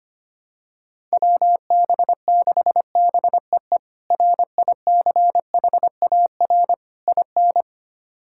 Morse code, heard as a keyed tone, about 25 wpm, 700 Hz. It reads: WB6BEE RICHAR IN
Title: K1USN SST QSO Practice - 25wpm